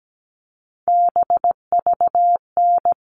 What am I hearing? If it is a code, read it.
Morse code, heard as a keyed tone, about 17 wpm, 700 Hz. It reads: BVN
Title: All Cut Numbers - 3-Digit 17wpm